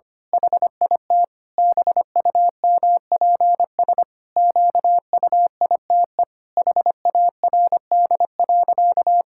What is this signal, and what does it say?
Morse code, heard as a keyed tone, about 25 wpm, 700 Hz. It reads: HIT BUMPS QUITE HARD.
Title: Pangram Challenge v2 - 25wpm